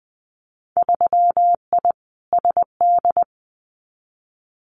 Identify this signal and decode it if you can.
Morse code, heard as a keyed tone, about 20 wpm, 700 Hz. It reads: 3I SD
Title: Winter Field Day Exchanges Encore 20wpm